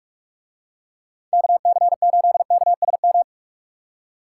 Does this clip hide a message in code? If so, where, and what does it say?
Yes: Morse code from 1.3 to 3.2 s, heard as a keyed tone, about 42 wpm, 700 Hz. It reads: KC8KSM